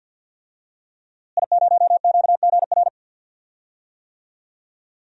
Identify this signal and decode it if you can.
Morse code, heard as a keyed tone, about 47 wpm, 700 Hz. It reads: I0QGR
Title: DXCC-100 Call Signs - 5 Characters - Repeated 2x - 50wpm